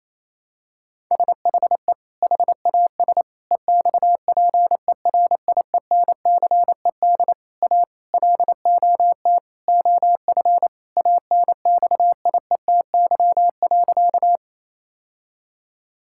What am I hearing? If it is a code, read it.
Morse code, heard as a keyed tone, about 28 wpm, 700 Hz. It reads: SHE HAS EXPERIENCED A LOT OF ANXIETY.
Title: Sentences from Top 1600 Words 28wpm